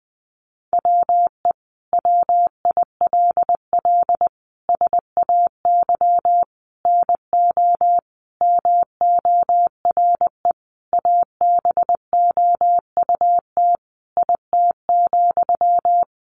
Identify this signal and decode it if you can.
Morse code, heard as a keyed tone, about 19 wpm, 700 Hz. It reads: WE WILL SAY NO MORE ABOUT IT,
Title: Sentences from Top 200 Words 20wpm